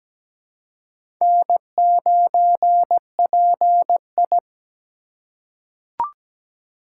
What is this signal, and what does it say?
Morse code, heard as a keyed tone, about 17 wpm, 700 Hz. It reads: N9PI
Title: USA Call Signs - 4x Character Spacing - 17wpm